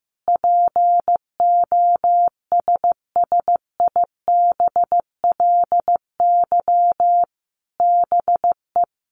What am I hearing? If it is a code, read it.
Morse code, heard as a keyed tone, about 15 wpm, 700 Hz. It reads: POSSIBLY BE
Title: Sentences from Top 1300 Words 15wpm